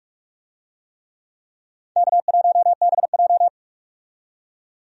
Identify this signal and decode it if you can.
Morse code, heard as a keyed tone, about 41 wpm, 700 Hz. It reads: K1BJ
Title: USA Call Signs - 4 Characters - 45wpm